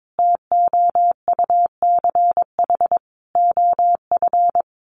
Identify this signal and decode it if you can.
Morse code, heard as a keyed tone, about 22 wpm, 700 Hz. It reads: TOUCH OF
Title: A Sound Of Thunder 22wpm